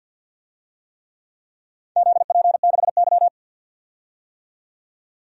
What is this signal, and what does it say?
Morse code, heard as a keyed tone, about 53 wpm, 700 Hz. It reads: ZP6Y
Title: DXCC-100 Call Signs - 4 Characters - 50wpm